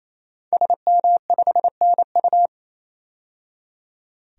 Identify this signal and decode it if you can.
Morse code, heard as a keyed tone, about 28 wpm, 700 Hz. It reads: SM5NU
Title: DXCC-100 Call Signs - 5 Characters - 28wpm